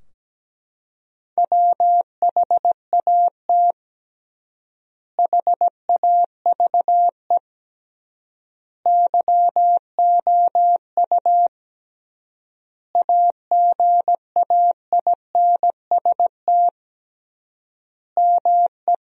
Morse code, heard as a keyed tone, about 17 wpm, 700 Hz. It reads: WHAT HAVE YOU AGAINST ME
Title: 3x Word-Spacing - Sentences from Top 400 Words - 17wpm